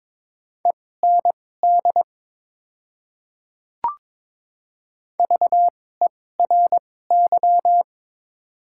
Morse code, heard as a keyed tone, about 22 wpm, spaced as for 16 wpm, 690 Hz. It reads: END VERY
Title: 2x Character Spacing - Top 300 Words - 22wpm